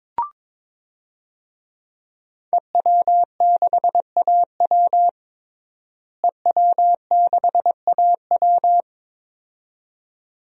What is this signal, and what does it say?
Morse code, heard as a keyed tone, about 22 wpm, 700 Hz. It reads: EW6AW EW6AW
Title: DXCC-100 Call Signs - 5 Characters - Repeated 2x - 22wpm